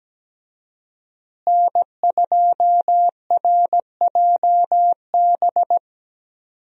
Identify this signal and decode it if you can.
Morse code, heard as a keyed tone, about 17 wpm, 700 Hz. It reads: N2RJB
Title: USA Call Signs - 5 Characters - Repeated 2x - 17wpm